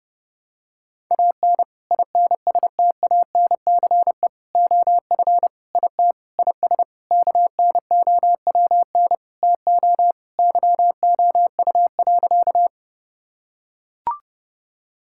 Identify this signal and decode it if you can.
Morse code, heard as a keyed tone, about 30 wpm, 700 Hz. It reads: AN INSTANCE OF IT IS KNOWN TO YOU.
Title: Sentences from Top 1000 Words - Review Part 5 - 30wpm